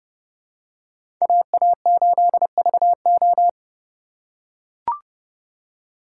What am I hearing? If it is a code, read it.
Morse code, heard as a keyed tone, about 30 wpm, 700 Hz. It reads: AA8VO
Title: USA Call Signs - 5 Characters - 2x Character Spacing - 30wpm